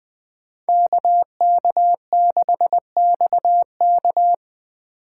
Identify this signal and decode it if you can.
Morse code, heard as a keyed tone, about 20 wpm, 700 Hz. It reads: KK6XK